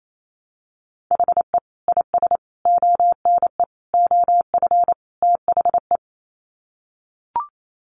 Morse code, heard as a keyed tone, about 28 wpm, 700 Hz. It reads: HE IS ONE OF THE
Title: Sets of 5 Words Encore 28wpm